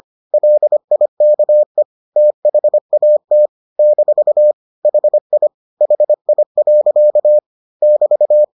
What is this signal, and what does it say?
Morse code, heard as a keyed tone, about 25 wpm, 600 Hz. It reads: LIKE THAT - HI HI. <BT>